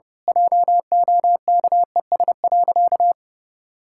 Morse code, heard as a keyed tone, about 30 wpm, 700 Hz. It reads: JOKES.
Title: Pangram Challenge v2 - 30wpm